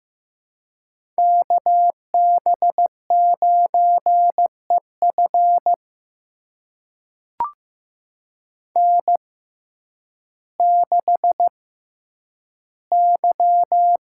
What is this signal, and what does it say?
Morse code, heard as a keyed tone, about 15 wpm, spaced as for 9 wpm, 700 Hz. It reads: KB9EF N6Y
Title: USA Call Signs - 5 Characters - 6x Character Spacing - 15wpm